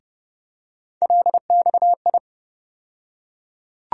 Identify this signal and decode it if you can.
Morse code, heard as a keyed tone, about 30 wpm, 700 Hz. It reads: LXI
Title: USA Call Sign Postfixes - 3 Letters 30wpm